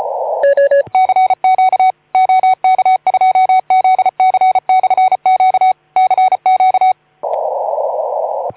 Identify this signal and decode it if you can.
Morse code, heard as a keyed tone, about 34 wpm, 2250 Hz. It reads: CQ OK2ZC/Q CQ
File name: Archiv / CUC181 / OK2ZC CUC 181